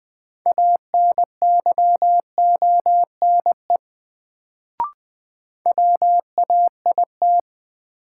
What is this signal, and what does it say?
Morse code, heard as a keyed tone, about 20 wpm, 700 Hz. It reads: ANYONE WAIT
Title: Sets of 5 Words Encore - New Word Drill - 20wpm